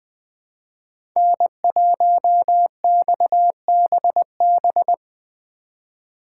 Morse code, heard as a keyed tone, about 20 wpm, 700 Hz. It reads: N1XBB
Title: USA Call Signs - 4x Character Spacing - 20wpm